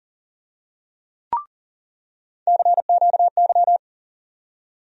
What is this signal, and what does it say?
Morse code, heard as a keyed tone, about 40 wpm, 700 Hz. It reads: CQY
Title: USA Call Sign Postfixes - 3 Letters 40wpm